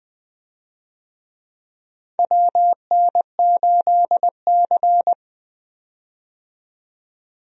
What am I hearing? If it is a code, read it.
Morse code, heard as a keyed tone, about 21 wpm, 700 Hz. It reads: WN8C
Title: USA Call Signs - 4 Characters - 20wpm